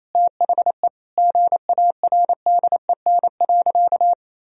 Morse code, heard as a keyed tone, about 28 wpm, 700 Hz. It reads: THE GARDEN.
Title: Sentences from Top 1000 Words - Review Part 4 - 28wpm